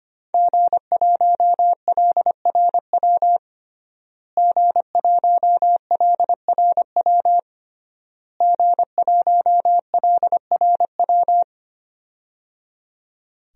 Morse code, heard as a keyed tone, about 25 wpm, 700 Hz. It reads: G1LRW G1LRW G1LRW
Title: DXCC-100 Call Signs - 5 Characters - Repeated 3x - 25wpm